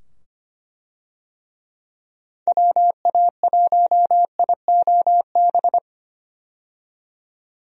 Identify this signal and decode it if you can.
Morse code, heard as a keyed tone, about 25 wpm, 700 Hz. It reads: WA1IOB
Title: USA Call Signs - 6 Characters - 25wpm